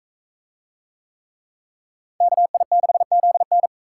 Morse code, heard as a keyed tone, about 41 wpm, 700 Hz. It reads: KI6ZN